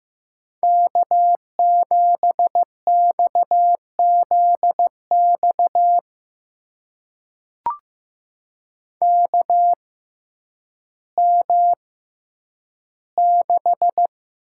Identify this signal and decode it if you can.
Morse code, heard as a keyed tone, about 15 wpm, spaced as for 9 wpm, 700 Hz. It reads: K7XZX KM6
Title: USA Call Signs - 6x Character Spacing - 15wpm